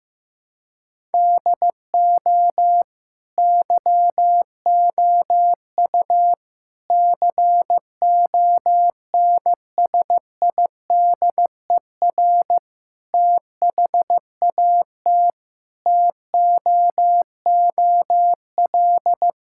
Morse code, heard as a keyed tone, about 15 wpm, 700 Hz. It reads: DO YOU CONSIDER THAT TOOL
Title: Sentences from Top 1000 Words - Review Part 6 - 15wpm